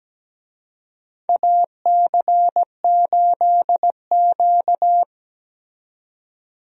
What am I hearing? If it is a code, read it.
Morse code, heard as a keyed tone, about 17 wpm, 700 Hz. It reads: AC8Q
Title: USA Call Signs - 4 Characters - 17wpm